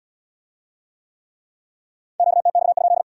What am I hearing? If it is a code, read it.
Morse code, heard as a keyed tone, about 75 wpm, 700 Hz. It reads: QTX?